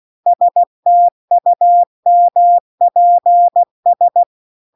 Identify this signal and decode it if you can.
Morse code, heard as a keyed tone, about 16 wpm, 700 Hz. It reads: STUMPS